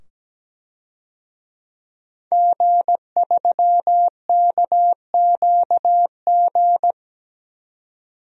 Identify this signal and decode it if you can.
Morse code, heard as a keyed tone, about 17 wpm, 700 Hz. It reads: G3KQG
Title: DXCC-100 Call Signs - 5 Characters - 17wpm